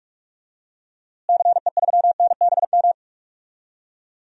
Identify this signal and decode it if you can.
Morse code, heard as a keyed tone, about 46 wpm, 700 Hz. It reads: CE3NBM
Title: DXCC-100 Call Signs - 6 Characters - Repeated 3x - 45wpm